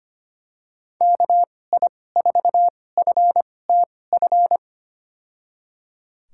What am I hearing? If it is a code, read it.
Morse code, heard as a keyed tone, about 25 wpm, spaced as for 18 wpm, 700 Hz. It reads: KI4FTF
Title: USA Call Signs - 6 Characters - 2x Character Spacing - 25wpm